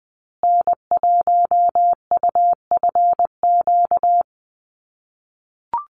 Morse code, heard as a keyed tone, about 20 wpm, 700 Hz. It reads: N1UFQ
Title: USA Call Signs Repeated 3x 20wpm